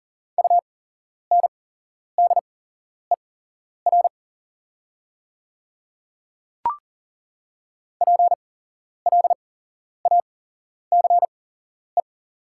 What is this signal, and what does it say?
Morse code, heard as a keyed tone, about 40 wpm, spaced as for 11 wpm, 700 Hz. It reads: UNDER PLACE
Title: 8x Character Spacing - Top 200 Words - 40wpm